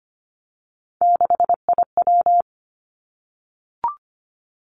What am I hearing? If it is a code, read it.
Morse code, heard as a keyed tone, about 25 wpm, 700 Hz. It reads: 6IW
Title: Three Challenging Letter-Number Combinations 25wpm